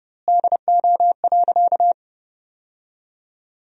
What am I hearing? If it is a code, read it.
Morse code, heard as a keyed tone, about 32 wpm, 700 Hz. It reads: DO.
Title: 2x Word-Spacing - Sentences from Top 100 Words - 30wpm